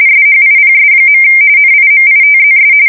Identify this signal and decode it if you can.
RTTY, about 45.45 baud. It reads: SHIPS. NIMITZS
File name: fsk170 / USS-New-Jersey